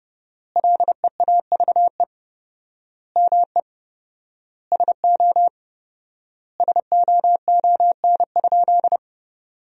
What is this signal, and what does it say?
Morse code, heard as a keyed tone, about 32 wpm, 700 Hz. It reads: LEAVE ME SO SOON?
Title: 4x Word-Spacing - Sentences from Top 400 Words - 30wpm